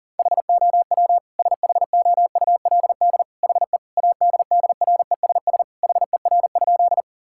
Morse code, heard as a keyed tone, about 40 wpm, 700 Hz. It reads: HOW SHOULD HE ADDRESS HER?
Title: Sentences from Top 1000 Words - Review Part 6 - 40wpm